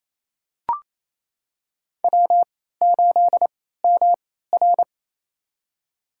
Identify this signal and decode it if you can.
Morse code, heard as a keyed tone, about 27 wpm, spaced as for 16 wpm, 700 Hz. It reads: W8MR
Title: USA Call Signs - 3x Character Spacing - 28wpm